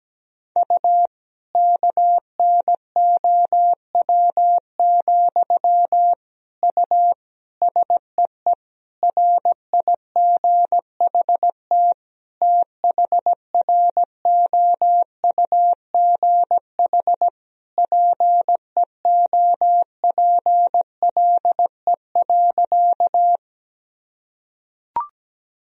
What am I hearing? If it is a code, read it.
Morse code, heard as a keyed tone, about 17 wpm, 700 Hz. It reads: U KNOW, U SEE RIGHT THROUGH PEOPLE.
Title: Sentences from Top 200 Words and Abbreviations 17wpm